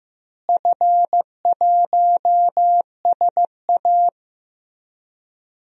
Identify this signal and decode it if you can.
Morse code, heard as a keyed tone, about 15 wpm, 700 Hz. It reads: F1SA
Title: DXCC-100 Call Signs - 4 Characters - Repeated 2x - 15wpm